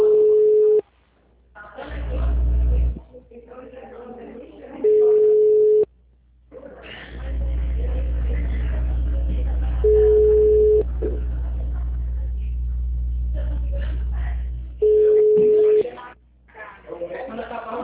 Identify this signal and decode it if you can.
Ringback tone (Europe, 425 Hz), with speech.